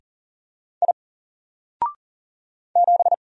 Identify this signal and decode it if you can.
Morse code, heard as a keyed tone, about 39 wpm, 700 Hz. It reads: I 7
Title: Single Character (Letter:Number:Punctuation:Prosign) Alt 40wpm